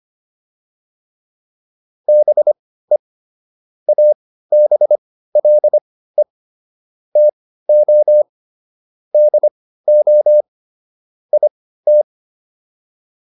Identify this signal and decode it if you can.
Morse code, heard as a keyed tone, about 25 wpm, spaced as for 15 wpm, 600 Hz. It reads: BE ABLE TO DO IT